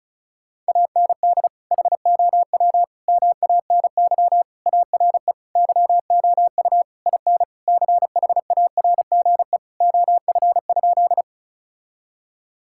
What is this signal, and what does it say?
Morse code, heard as a keyed tone, about 34 wpm, 700 Hz. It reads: AND HOW MANY ARE YOU IN CHARGE OF?
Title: Sentences from Top 1000 Words - Review Part 6 - 35wpm